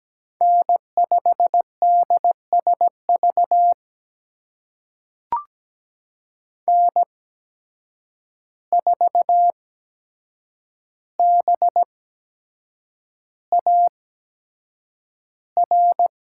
Morse code, heard as a keyed tone, about 17 wpm, spaced as for 7 wpm, 700 Hz. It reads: N5DSV N4BAR
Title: USA Call Signs - 5 Characters - 8x Character Spacing - 17wpm